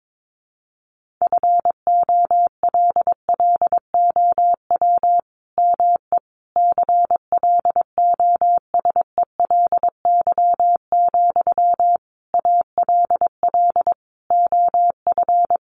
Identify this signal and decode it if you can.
Morse code, heard as a keyed tone, about 22 wpm, 700 Hz. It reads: FOLLOW ME CLOSELY, ALL OF
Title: Sentences from Top 1400 Words 22wpm